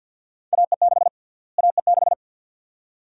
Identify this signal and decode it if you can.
Morse code, heard as a keyed tone, about 51 wpm, 700 Hz. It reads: AE6 AE6
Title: USA Call Sign Prefixes - Repeated 2x - 50wpm